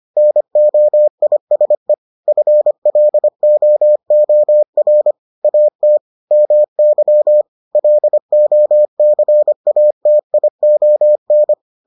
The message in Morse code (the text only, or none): NOISE FLOOR AT MY LOCATION